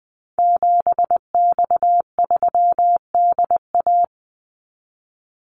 Morse code, heard as a keyed tone, about 20 wpm, 700 Hz. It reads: 7X3DA
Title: Call Signs Repeated 3x - 20wpm